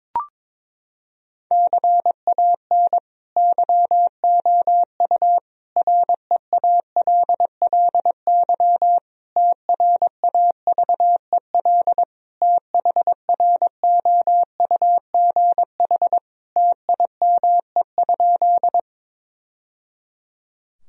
Morse code, heard as a keyed tone, about 22 wpm, 700 Hz. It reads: CAN YOU REALLY TRAVEL THROUGH TIME?
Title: Sentences from Top 1000 Words - Review Part 2 - 22wpm